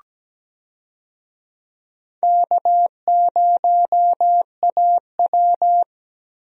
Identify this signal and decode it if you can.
Morse code, heard as a keyed tone, about 17 wpm, 700 Hz. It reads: K0AW
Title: USA Call Signs - 4 Characters - Repeated 3x - 17wpm